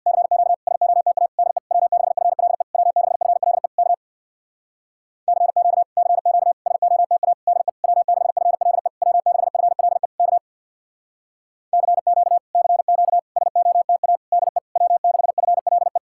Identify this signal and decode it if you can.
Morse code, heard as a keyed tone, about 66 wpm, 700 Hz. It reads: CQ SOTA DE W6FLE W6FLE K CQ CQ SOTA DE W6FLE W6FLE K CQ CQ SOTA DE W6FLE